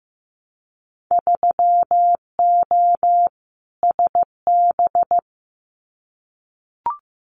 Morse code, heard as a keyed tone, about 15 wpm, 700 Hz. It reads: 3O SB
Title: Winter Field Day Exchanges Encore 15wpm